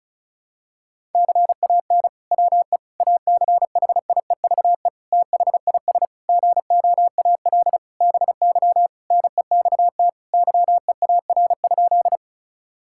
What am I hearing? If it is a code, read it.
Morse code, heard as a keyed tone, about 36 wpm, 700 Hz. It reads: CAN WE ACHIEVE THIS GOAL BY NEXT YEAR?
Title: Sentences from Top 1000 Words - Review Part 5 - 35wpm